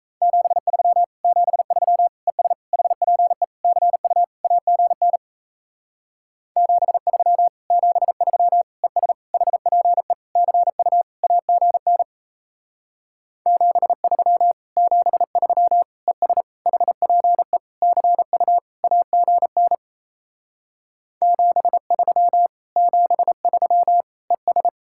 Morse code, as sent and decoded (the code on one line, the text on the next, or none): --... ...-- / --... ...-- / . ... / .... .--. / -.-. ..- / .- --. -. / --... ...-- / --... ...-- / . ... / .... .--. / -.-. ..- / .- --. -. / --... ...-- / --... ...-- / . ... / .... .--. / -.-. ..- / .- --. -. / --... ...-- / --... ...-- / . ...
73 73 ES HPE CU AGN 73 73 ES HPE CU AGN 73 73 ES HPE CU AGN 73 73 ES